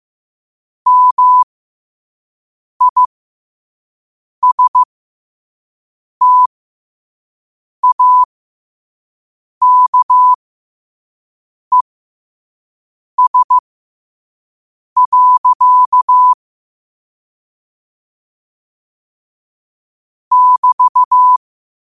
Morse code, heard as a keyed tone, about 15 wpm, spaced as for 5 wpm, 1000 Hz. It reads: MISTAKES. <BT>
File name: qotdfiles / 5wpm